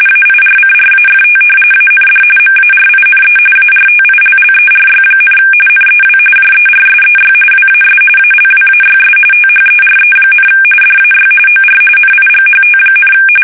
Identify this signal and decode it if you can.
RTTY, about 45.45 baud. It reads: AIRCRAFT AND MARINE ARTILLERY, NEW JERSEY WAS ALWAYS THERE, DAY OR NIGHT,
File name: fsk850 / battleshipnj